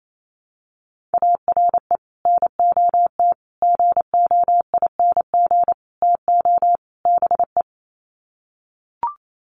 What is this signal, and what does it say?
Morse code, heard as a keyed tone, about 28 wpm, 700 Hz. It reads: ARE NOT GOING TO BE
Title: Sets of 5 Words Encore 28wpm